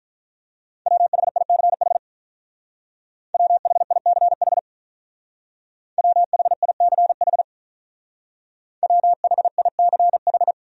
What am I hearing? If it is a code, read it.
Morse code, heard as a keyed tone, about 45 wpm, 700 Hz. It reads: WHICH WHICH WHICH WHICH